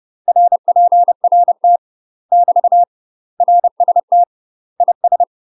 Morse code, heard as a keyed tone, about 30 wpm, 700 Hz. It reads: RPRT <BT> RST IS
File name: UK MASTER QSOs TRAINING / 30WPM / UK MASTER QSOs 0002